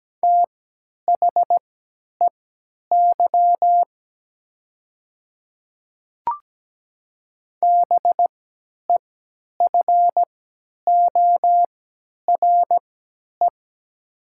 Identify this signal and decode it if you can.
Morse code, heard as a keyed tone, about 17 wpm, spaced as for 10 wpm, 700 Hz. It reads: THEY BEFORE
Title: 3x Character Spacing - Top 200 Words - 17wpm